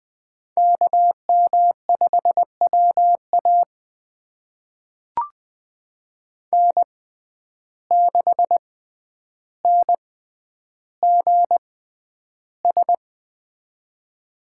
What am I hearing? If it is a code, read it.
Morse code, heard as a keyed tone, about 20 wpm, spaced as for 10 wpm, 700 Hz. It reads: KM5WA N6NGS
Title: USA Call Signs - 5 Characters - 6x Character Spacing - 20wpm